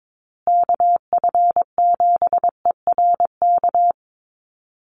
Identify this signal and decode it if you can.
Morse code, heard as a keyed tone, about 22 wpm, 700 Hz. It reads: KF7ERK
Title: USA Call Signs Repeated 3x 22wpm